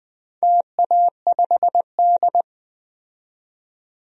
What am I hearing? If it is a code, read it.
Morse code, heard as a keyed tone, about 20 wpm, 700 Hz. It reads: TA5D
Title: DXCC-100 Call Signs - 4 Characters - 20wpm